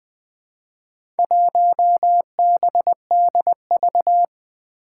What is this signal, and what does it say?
Morse code, heard as a keyed tone, about 20 wpm, 700 Hz. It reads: 1BDV
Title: Four Challenging Letter-Number Combinations 20wpm